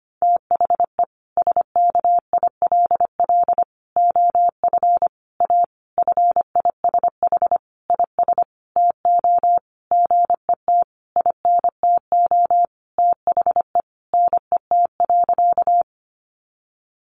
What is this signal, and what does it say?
Morse code, heard as a keyed tone, about 25 wpm, 700 Hz. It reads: THE SKILL OF A FISH IS TO GET INTO THE NET.
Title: Sentences from Top 1400 Words 25wpm